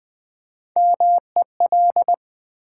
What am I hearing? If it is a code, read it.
Morse code, heard as a keyed tone, about 20 wpm, 700 Hz. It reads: MEL
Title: Build A Word - 1 Letter At A Time - 4-Letter Words - 20wpm